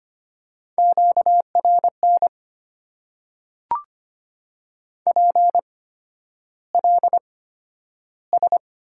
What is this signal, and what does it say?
Morse code, heard as a keyed tone, about 25 wpm, spaced as for 11 wpm, 700 Hz. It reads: QRN PLS
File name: Top 90 QSO Elements - 8x Character Spacing - 25wpm